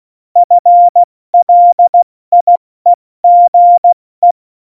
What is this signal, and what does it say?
Morse code, heard as a keyed tone, about 16 wpm, 700 Hz. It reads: FLIEGE
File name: Morsekurs / Witze / Witz 22. Oktober 2019, 70 F80